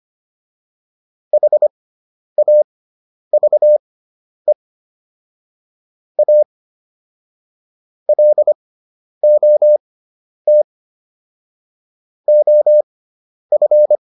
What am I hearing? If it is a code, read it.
Morse code, heard as a keyed tone, about 25 wpm, spaced as for 10 wpm, 600 Hz. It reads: HAVE A LOT OF